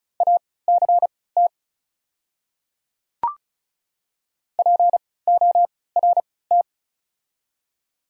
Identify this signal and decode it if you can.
Morse code, heard as a keyed tone, about 36 wpm, spaced as for 20 wpm, 700 Hz. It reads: ACT PORT
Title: 3x Character Spacing - Top 200 Words - 35wpm